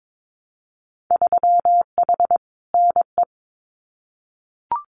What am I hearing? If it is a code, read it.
Morse code, heard as a keyed tone, about 22 wpm, 700 Hz. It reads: 3H NE
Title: Winter Field Day Exchanges Encore 22wpm